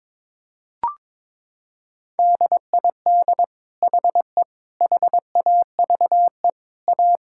Morse code, heard as a keyed tone, about 22 wpm, 700 Hz. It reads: DID HE HAVE A
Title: Sentences from Top 1000 Words - Review Part 6 - 22wpm